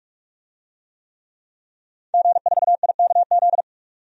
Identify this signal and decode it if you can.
Morse code, heard as a keyed tone, about 46 wpm, 700 Hz. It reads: G4IKZ
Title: DXCC-100 Call Signs - 5 Characters - 45wpm